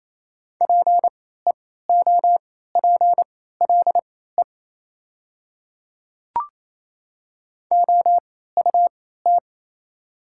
Morse code, heard as a keyed tone, about 28 wpm, spaced as for 16 wpm, 700 Hz. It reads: PEOPLE OUT
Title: 3x Character Spacing - Top 200 Words - 28wpm